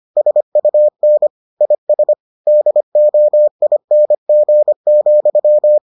Morse code, heard as a keyed tone, about 25 wpm, 600 Hz. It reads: SUN IS DOING,